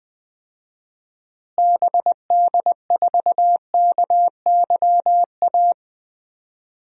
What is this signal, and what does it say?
Morse code, heard as a keyed tone, about 20 wpm, 700 Hz. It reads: BD4KYA
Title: DXCC-100 Call Signs - 6 Characters - Repeated 2x - 20wpm